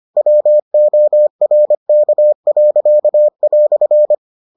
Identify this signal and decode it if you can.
Morse code, heard as a keyed tone, about 25 wpm, 600 Hz. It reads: WORK."